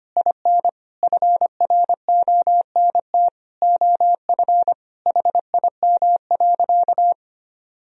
Morse code, heard as a keyed tone, about 25 wpm, 700 Hz. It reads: IN FRONT OF HIM.
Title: Sentences from Top 1000 Words - Review Part 2 - 25wpm